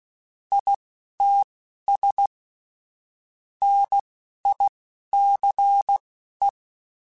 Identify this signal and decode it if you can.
Morse code, heard as a keyed tone, about 16 wpm, spaced as for 11 wpm, 800 Hz. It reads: ITS NICE
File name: QSOs / QSOs-mp3s / WD0BGP 2@10